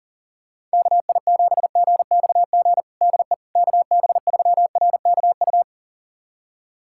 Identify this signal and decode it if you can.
Morse code, heard as a keyed tone, about 40 wpm, 700 Hz. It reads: KI7GXG DE KB3RKU
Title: USA Callsign DE Callsign 40wpm